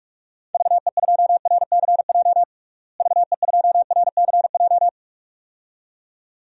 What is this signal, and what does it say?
Morse code, heard as a keyed tone, about 41 wpm, 700 Hz. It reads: VE2RCJ VE2RCJ
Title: DXCC-100 Call Signs - 6 Characters - Repeated 2x - 45wpm